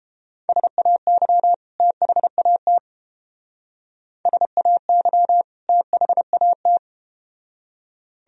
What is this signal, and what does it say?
Morse code, heard as a keyed tone, about 31 wpm, 700 Hz. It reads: SAY THAT SAY THAT